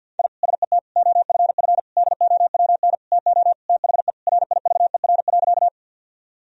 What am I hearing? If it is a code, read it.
Morse code, heard as a keyed tone, about 53 wpm, 700 Hz. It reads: I SET OFF DOWN TO THE RIVER.
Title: Sentences from Top 1000 Words - Review Part 1 - 50wpm